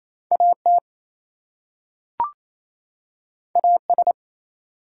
Morse code, heard as a keyed tone, about 27 wpm, 700 Hz. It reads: AT AS
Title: Two Letter Words - 28wpm